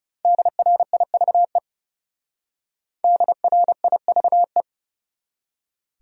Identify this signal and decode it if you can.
Morse code, heard as a keyed tone, about 34 wpm, 700 Hz. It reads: DRIVE DRIVE
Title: Speed-Racing Top 500 Words 30wpm